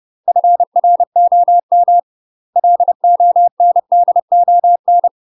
Morse code, heard as a keyed tone, about 30 wpm, 700 Hz. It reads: FROM LONDON
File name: UK MASTER QSOs TRAINING / 30WPM / UK MASTER QSOs 0003